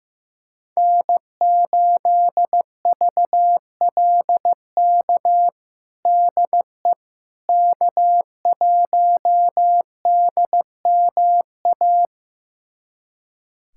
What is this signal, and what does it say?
Morse code, heard as a keyed tone, about 15 wpm, 700 Hz. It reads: N8VLK DE K1DMA
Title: USA Callsign DE Callsign - 5 Characters - 15wpm